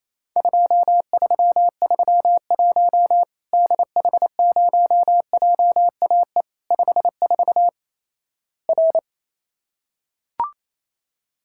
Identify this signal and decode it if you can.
Morse code, heard as a keyed tone, about 28 wpm, 700 Hz. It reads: 2331 DH0JAE 54 R
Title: Worked All Europe DX Contest - QTC Exchange - 28wpm